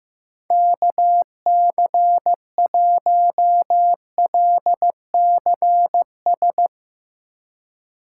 Morse code, heard as a keyed tone, about 16 wpm, 700 Hz. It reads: KC1LCS
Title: USA Call Signs - 2x Character Spacing - 15wpm